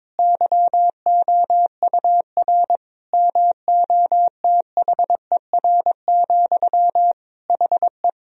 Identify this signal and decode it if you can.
Morse code, heard as a keyed tone, about 22 wpm, 700 Hz. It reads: YOUR MOTHER, HE
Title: Sentences from Top 1000 Words - Review Part 4 - 22wpm